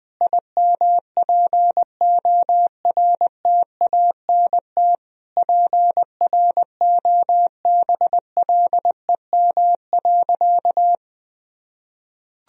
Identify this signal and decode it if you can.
Morse code, heard as a keyed tone, about 21 wpm, 700 Hz. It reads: IMPORTANT PROBLEM.
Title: Sentences from Top 1000 Words - Review Part 2 - 20wpm